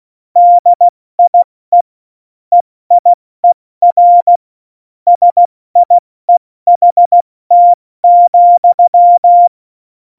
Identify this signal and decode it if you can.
Morse code, heard as a keyed tone, about 16 wpm, 700 Hz. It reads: DIE EIER SIEHT,
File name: Morsekurs / Witze / Witz 28 April 2020, 70 F80